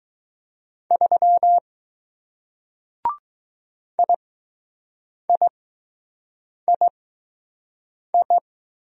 Morse code, heard as a keyed tone, about 20 wpm, spaced as for 6 wpm, 700 Hz. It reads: 3 IIII